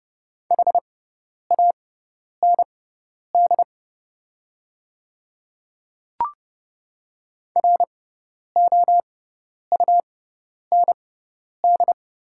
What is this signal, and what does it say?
Morse code, heard as a keyed tone, about 30 wpm, spaced as for 10 wpm, 700 Hz. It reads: HAND ROUND
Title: 6x Character Spacing - Top 500 Words - 30wpm